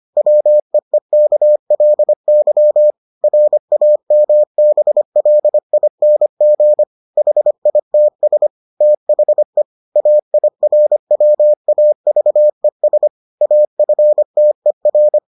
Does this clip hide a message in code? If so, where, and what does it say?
Yes: Morse code from 0.2 to 15.2 s, heard as a keyed tone, about 25 wpm, 600 Hz. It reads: WEEKLY RAMBLING HITS THE AIRWAVES AFTER